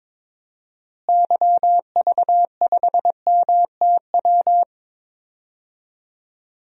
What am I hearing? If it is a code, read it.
Morse code, heard as a keyed tone, about 22 wpm, 700 Hz. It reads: YV5MTW
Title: DXCC-100 Call Signs - 6 Characters - Repeated 2x - 22wpm